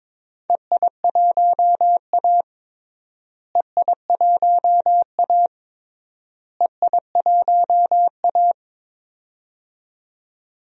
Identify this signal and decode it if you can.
Morse code, heard as a keyed tone, about 22 wpm, 700 Hz. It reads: EI1A EI1A EI1A
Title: DXCC-100 Call Signs - 4 Characters - Repeated 3x - 22wpm